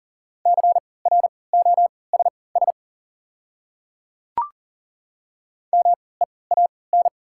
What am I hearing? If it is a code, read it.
Morse code, heard as a keyed tone, about 38 wpm, spaced as for 22 wpm, 700 Hz. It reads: CROSS MEAN
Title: 3x Character Spacing - Top 300 Words - 40wpm